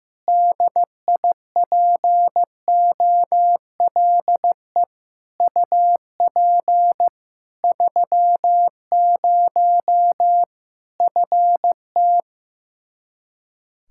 Morse code, heard as a keyed tone, about 15 wpm, 700 Hz. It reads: DIPOLE UP 30 FT
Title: K1USN SST QSO Practice 15wpm